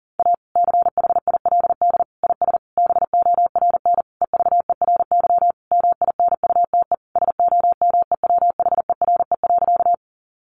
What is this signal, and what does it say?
Morse code, heard as a keyed tone, about 40 wpm, 700 Hz. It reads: A CHILD IS BORN EVERY MINUTE SOMEWHERE.
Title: Sentences from Top 1400 Words 40wpm